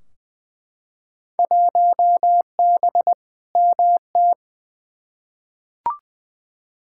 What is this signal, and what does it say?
Morse code, heard as a keyed tone, about 20 wpm, 700 Hz. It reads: 1B MT
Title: Field Day Exchanges - 20wpm